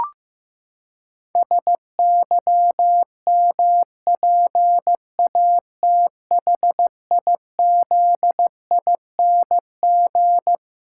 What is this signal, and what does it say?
Morse code, heard as a keyed tone, about 15 wpm, 700 Hz. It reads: SYMPATHIZING
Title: Pangram Challenge v2 - 15wpm